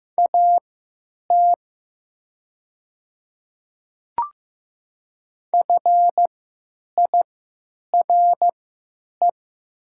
Morse code, heard as a keyed tone, about 15 wpm, spaced as for 9 wpm, 700 Hz. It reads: AT FIRE
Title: 3x Character Spacing - Top 500 Words - 15wpm